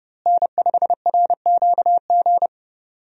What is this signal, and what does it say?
Morse code, heard as a keyed tone, about 30 wpm, 700 Hz. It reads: N5RQG
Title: USA Call Signs - 5 Characters - Repeated 3x - 30wpm